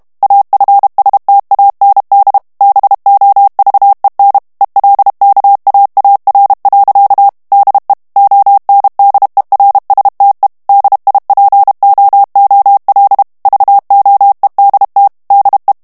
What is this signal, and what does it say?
Morse code, heard as a keyed tone, about 32 wpm, 800 Hz. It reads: AFSTAND BOVEN ELKAAR. DE ONDERSTE DIPOOL VOEDT DE
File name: morse mp3 / HOLLANDAIS texte clair 1 vitesse 30 mots minute